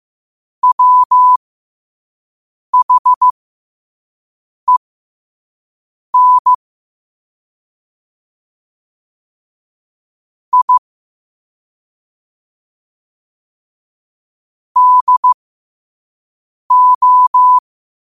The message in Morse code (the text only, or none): WHEN I DO